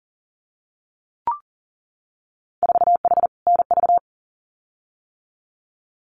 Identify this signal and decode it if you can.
Morse code, heard as a keyed tone, about 38 wpm, 700 Hz. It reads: 4H NV
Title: Winter Field Day Exchanges Encore 40wpm